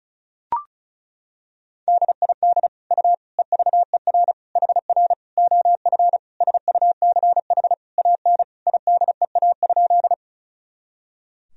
Morse code, heard as a keyed tone, about 36 wpm, 700 Hz. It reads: DID U EVER HR OF SUCH AN IDEA?
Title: Sentences from Top 400 Words and Abbreviations 35wpm